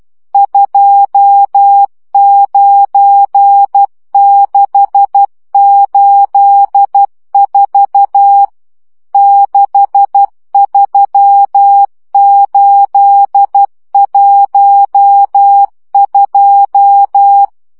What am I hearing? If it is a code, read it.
Morse code, heard as a keyed tone, about 12 wpm, 800 Hz. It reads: 29684 63812
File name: morze vizsga 2011-1 10